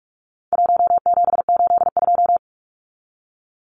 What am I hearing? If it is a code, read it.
Morse code, heard as a keyed tone, about 44 wpm, 700 Hz. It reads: 1782
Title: Numbers - 4-Digit 45wpm